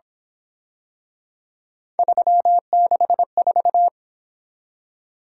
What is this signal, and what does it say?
Morse code, heard as a keyed tone, about 27 wpm, 700 Hz. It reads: 364
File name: Speed-Racing - Three Letter-Number Combinations - 22wpm